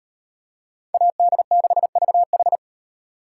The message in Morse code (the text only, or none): AD6VH